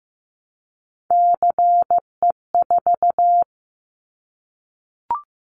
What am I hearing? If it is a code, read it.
Morse code, heard as a keyed tone, about 15 wpm, 700 Hz. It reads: CE4
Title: Callsign Prefixes for Top 100 DXCC - 15wpm